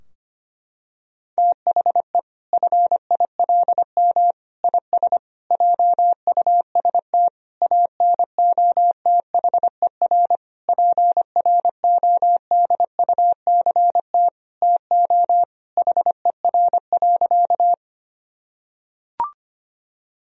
Morse code, heard as a keyed tone, about 25 wpm, 700 Hz. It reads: THE FILM IS JUST ANOTHER PRODUCT TO HER.
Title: Sentences from Top 1000 Words - Review Part 3 - 25wpm